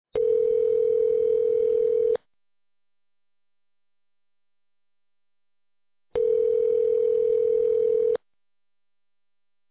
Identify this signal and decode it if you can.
Ringback tone (North America), 440 + 480 Hz.